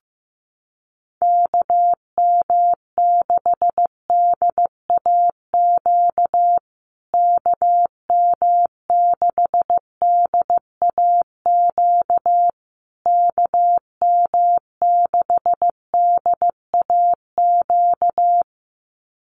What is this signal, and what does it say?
Morse code, heard as a keyed tone, about 15 wpm, 700 Hz. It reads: KM6DAQ KM6DAQ KM6DAQ
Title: USA Call Signs Repeated 3x 15wpm